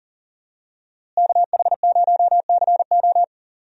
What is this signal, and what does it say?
Morse code, heard as a keyed tone, about 40 wpm, 700 Hz. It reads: KH0CO
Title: USA Call Signs - 5 Characters - Repeated 2x - 40wpm